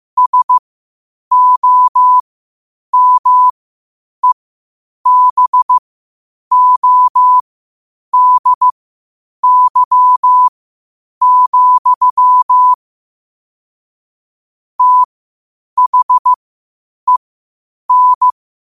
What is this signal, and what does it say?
Morse code, heard as a keyed tone, about 15 wpm, spaced as for 8 wpm, 1000 Hz. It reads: SOMEBODY, THEN
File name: qotdfiles / 8wpm